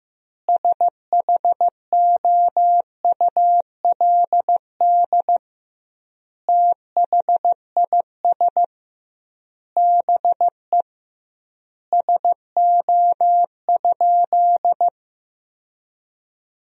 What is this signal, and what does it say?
Morse code, heard as a keyed tone, about 15 wpm, 700 Hz. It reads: SHOULD THIS BE SO?
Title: 2x Word-Spacing - Sentences from Top 300 Words - 15wpm